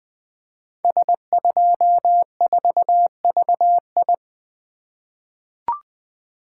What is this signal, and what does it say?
Morse code, heard as a keyed tone, about 20 wpm, 700 Hz. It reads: S24VI
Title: Code Groups - 5 Letter:Numbers - 20wpm